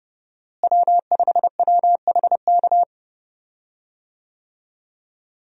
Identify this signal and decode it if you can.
Morse code, heard as a keyed tone, about 32 wpm, 700 Hz. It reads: W5WHK
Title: USA Call Signs - 5 Characters - Repeated 3x - 30wpm